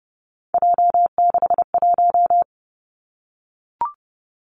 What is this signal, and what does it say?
Morse code, heard as a keyed tone, about 30 wpm, 700 Hz. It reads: J61
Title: Three Challenging Letter-Number Combinations 30wpm